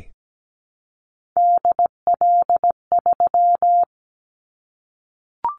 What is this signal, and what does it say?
Morse code, heard as a keyed tone, about 17 wpm, 700 Hz. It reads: DL3
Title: Callsign Prefixes for Top 50 DXCC - 17wpm